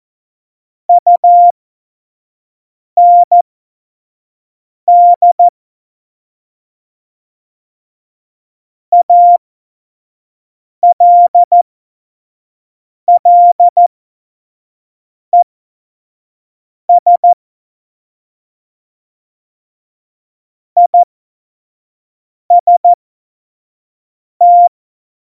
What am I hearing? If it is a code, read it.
Morse code, heard as a keyed tone, about 14 wpm, spaced as for 5 wpm, 700 Hz. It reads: UND ALLES IST
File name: Morsekurs / Witze / Witz 12. Januar 2021, 25 F70